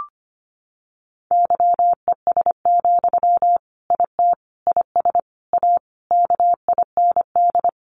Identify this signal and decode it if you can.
Morse code, heard as a keyed tone, about 25 wpm, 700 Hz. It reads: YES, IT IS A KIND
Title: Sentences from Top 1400 Words 25wpm